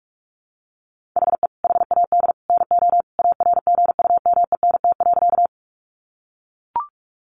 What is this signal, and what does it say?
Morse code, heard as a keyed tone, about 44 wpm, 700 Hz. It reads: HE HAD NO ARGUMENT.
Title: Sentences from Top 1400 Words 45wpm